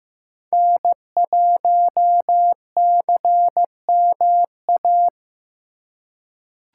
Morse code, heard as a keyed tone, about 15 wpm, 700 Hz. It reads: N1CMA
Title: USA Call Signs - 4x Character Spacing - 15wpm